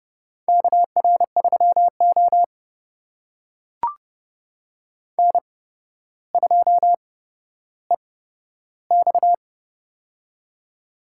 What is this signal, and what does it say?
Morse code, heard as a keyed tone, about 30 wpm, spaced as for 13 wpm, 700 Hz. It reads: KR3O N2EX
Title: USA Call Signs - 4 Characters - 8x Character Spacing - 30wpm